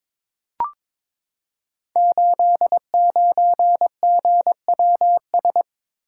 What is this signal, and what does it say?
Morse code, heard as a keyed tone, about 23 wpm, 700 Hz. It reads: 89GWS
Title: Code Groups - 5 Letter:Numbers - 22wpm